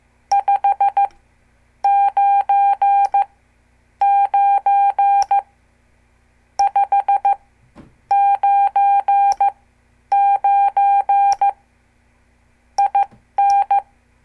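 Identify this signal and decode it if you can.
Morse code, heard as a keyed tone, about 15 wpm, spaced as for 9 wpm, 800 Hz. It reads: 599 599 IN
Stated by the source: Example CW QSO between N2APB and N2CX from a CWTD Morse lesson.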